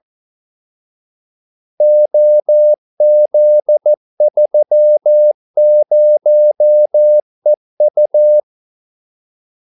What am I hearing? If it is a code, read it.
Morse code, heard as a keyed tone, about 14 wpm, 600 Hz. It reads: OZ30EU
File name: Callsigns-14wpm